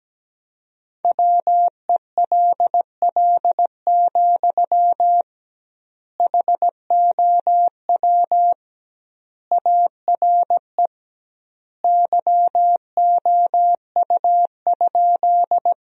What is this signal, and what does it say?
Morse code, heard as a keyed tone, about 17 wpm, 700 Hz. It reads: WELL, HOW ARE YOU?
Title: 2x Word-Spacing - Sentences from Top 200 Words - 17wpm